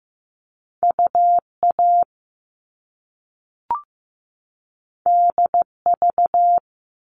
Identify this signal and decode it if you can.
Morse code, heard as a keyed tone, about 15 wpm, 700 Hz. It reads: UA DV
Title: All Cut Numbers - 2-Digit 15wpm